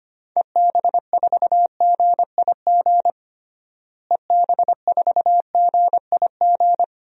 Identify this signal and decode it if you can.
Morse code, heard as a keyed tone, about 25 wpm, 700 Hz. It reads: EB4GIG EB4GIG
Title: DXCC-100 Call Signs - 6 Characters - Repeated 2x - 25wpm